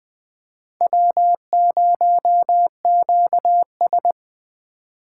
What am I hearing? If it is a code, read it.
Morse code, heard as a keyed tone, about 20 wpm, 700 Hz. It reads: W0QS
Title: Speed Racing - USA 1x2 Call Signs 17wpm